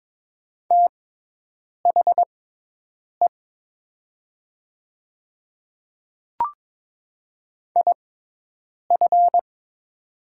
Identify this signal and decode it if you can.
Morse code, heard as a keyed tone, about 22 wpm, spaced as for 8 wpm, 700 Hz. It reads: THE IF
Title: 6x Character Spacing - Top 100 Words - 22wpm